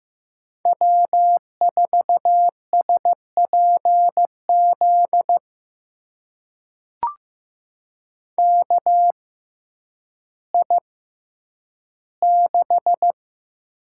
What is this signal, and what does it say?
Morse code, heard as a keyed tone, about 15 wpm, spaced as for 9 wpm, 700 Hz. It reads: W4SPZ KI6
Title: USA Call Signs - 5 Characters - 6x Character Spacing - 15wpm